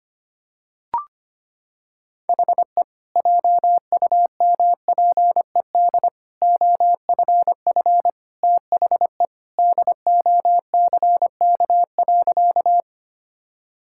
Morse code, heard as a keyed tone, about 25 wpm, 700 Hz. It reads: HE JUMPED OFF THE DOCK.
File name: practice / 25wpm / Sentences with 2-Word Phrases - 25wpm